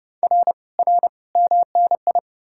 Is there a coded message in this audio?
Yes: Morse code from 0.2 to 2.2 s, heard as a keyed tone, about 40 wpm, 700 Hz. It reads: R R MNI